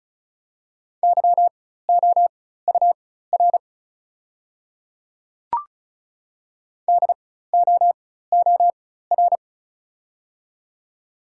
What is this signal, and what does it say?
Morse code, heard as a keyed tone, about 36 wpm, spaced as for 16 wpm, 690 Hz. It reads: YOUR DOOR
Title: 4x Character Spacing - Top 300 Words - 35wpm